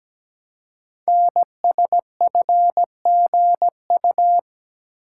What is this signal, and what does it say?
Morse code, heard as a keyed tone, about 17 wpm, 700 Hz. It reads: NSFGU
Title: Code Groups - 5 Letters - 17wpm